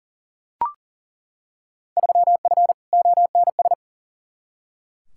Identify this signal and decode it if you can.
Morse code, heard as a keyed tone, about 40 wpm, 700 Hz. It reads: 3F ONS
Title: Field Day Exchanges - 40wpm